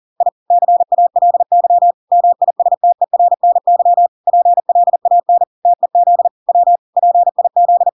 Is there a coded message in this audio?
Yes: Morse code from 0.2 to 7.9 s, heard as a keyed tone, about 40 wpm, 700 Hz. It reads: I CALY MISTERNY PLAN TEZ W PIZ